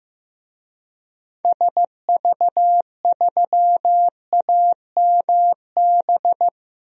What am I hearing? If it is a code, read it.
Morse code, heard as a keyed tone, about 16 wpm, 700 Hz. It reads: SV3AMB
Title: DXCC-100 Call Signs - 6 Characters - 15wpm